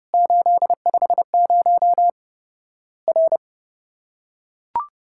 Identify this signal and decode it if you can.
Morse code, heard as a keyed tone, about 28 wpm, 700 Hz. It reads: 850 R
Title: Worked All Europe DX Contest - QTC Exchange - 30wpm